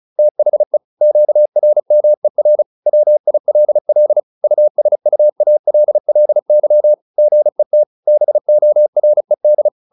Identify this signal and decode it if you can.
Morse code, heard as a keyed tone, about 35 wpm, 600 Hz. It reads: THE QRMER WILL USUALLY GET BORED
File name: SS309 35